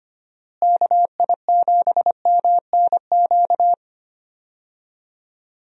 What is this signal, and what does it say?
Morse code, heard as a keyed tone, about 25 wpm, 700 Hz. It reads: KI7MNQ
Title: USA Call Signs - 6 Characters - Repeated 2x - 25wpm